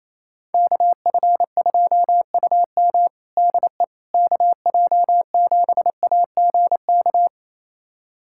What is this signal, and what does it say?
Morse code, heard as a keyed tone, about 28 wpm, 700 Hz. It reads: KF2UM DE KJ7AGK
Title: USA Callsign DE Callsign 28wpm